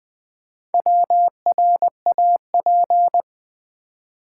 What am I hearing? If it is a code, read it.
Morse code, heard as a keyed tone, about 20 wpm, 700 Hz. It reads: WRAP
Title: Four Letter Words - 20wpm